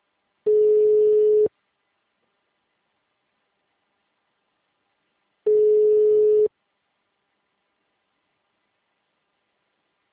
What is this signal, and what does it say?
Ringback tone (Europe, 425 Hz).